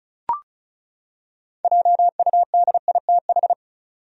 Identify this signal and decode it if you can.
Morse code, heard as a keyed tone, about 34 wpm, 700 Hz. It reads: JUDITH
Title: USA Common Names - 35wpm